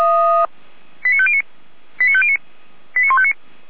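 Five-tone selective call (ZVEI), heard as quick runs of five tones, tones of about 70 ms. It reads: 89509 89509 89159
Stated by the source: ZVEI five-tone call received via SDR.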